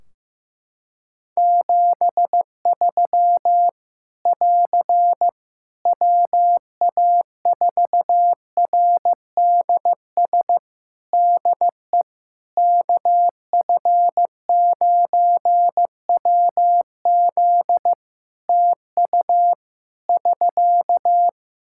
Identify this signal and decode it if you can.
Morse code, heard as a keyed tone, about 15 wpm, 700 Hz. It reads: 73 <AR> WA4RDS DE KF9WZ TU <SK>
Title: K1USN SST QSO Practice 15wpm